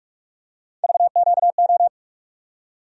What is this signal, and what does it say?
Morse code, heard as a keyed tone, about 45 wpm, 700 Hz. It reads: VQO
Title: USA Call Sign Postfixes - 3 Letters 45wpm